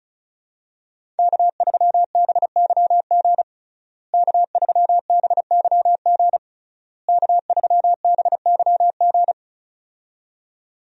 Morse code, heard as a keyed tone, about 34 wpm, 700 Hz. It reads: K3BYG K3BYG K3BYG
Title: USA Call Signs - 5 Characters - Repeated 3x - 35wpm